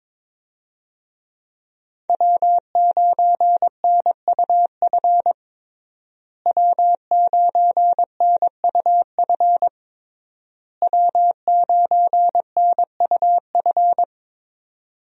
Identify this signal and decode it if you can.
Morse code, heard as a keyed tone, about 22 wpm, 700 Hz. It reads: W9NUF W9NUF W9NUF
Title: USA Call Signs - 5 Characters - Repeated 3x - 22wpm